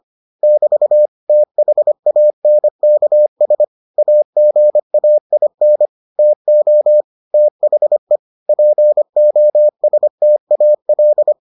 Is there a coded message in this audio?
Yes: Morse code from 0.4 to 11.3 s, heard as a keyed tone, about 25 wpm, 600 Hz. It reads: <BT> THANKS AGAIN TO THE POSTAL